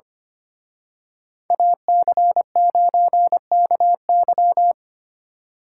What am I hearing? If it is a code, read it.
Morse code, heard as a keyed tone, about 31 wpm, 700 Hz. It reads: AC9KY
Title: Speed Racing - USA 2x2 Call Signs 25wpm